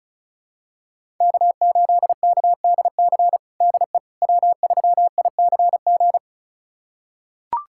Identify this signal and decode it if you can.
Morse code, heard as a keyed tone, about 36 wpm, 700 Hz. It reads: K8KDC DE W3ICG
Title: USA Callsign DE Callsign - 5 Characters - 35wpm